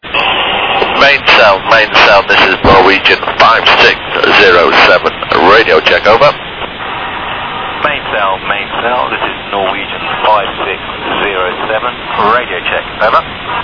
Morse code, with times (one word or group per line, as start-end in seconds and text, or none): none